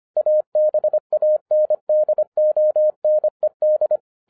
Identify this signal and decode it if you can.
Morse code, heard as a keyed tone, about 25 wpm, 610 Hz. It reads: ABANDONED
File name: 2011-07-mp3 / n39 THE 3.11 2 25wpm